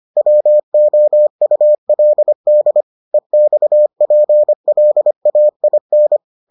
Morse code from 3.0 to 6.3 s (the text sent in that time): EXPLAIN